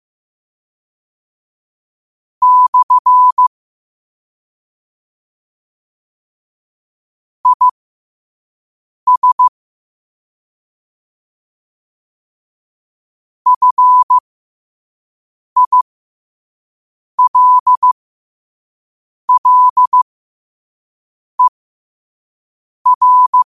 Morse code, heard as a keyed tone, about 15 wpm, spaced as for 5 wpm, 1000 Hz. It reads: / IS FILLER
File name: qotdfiles / 5wpm